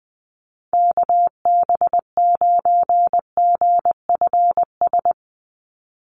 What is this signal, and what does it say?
Morse code, heard as a keyed tone, about 20 wpm, 700 Hz. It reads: KB9GFS
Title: USA Call Signs 20wpm